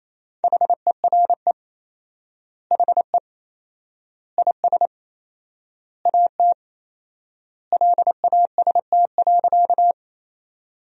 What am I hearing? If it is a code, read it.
Morse code, heard as a keyed tone, about 28 wpm, 700 Hz. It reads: HERE HE IS AT LAST.
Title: 4x Word-Spacing - Sentences from Top 300 Words - 28wpm